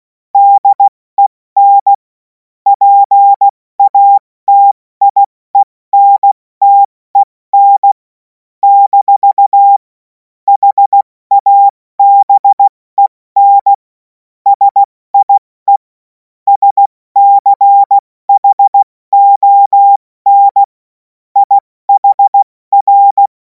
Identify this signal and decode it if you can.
Morse code, heard as a keyed tone, about 16 wpm, 800 Hz. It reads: DEN PATIENTEN - HABEN SIE SCHON IHR
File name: Morsekurs / Witze / Witz 07. Juli 2026 70 F80